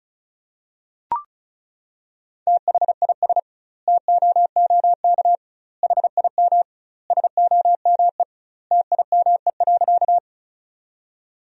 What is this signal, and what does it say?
Morse code, heard as a keyed tone, about 36 wpm, 700 Hz. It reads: THIS TOOK HIM SOME TIME.
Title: 2x Word-Spacing - Sentences from Top 300 Words - 35wpm